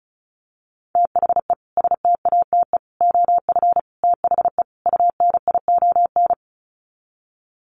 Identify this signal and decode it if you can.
Morse code, heard as a keyed tone, about 34 wpm, 700 Hz. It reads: THE STATE OF THE UNION
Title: Sets of 5 Words Encore 35wpm